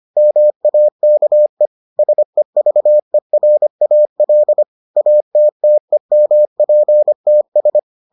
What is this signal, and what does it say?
Morse code, heard as a keyed tone, about 25 wpm, 600 Hz. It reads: MAKE SEVERAL ATTEMPTS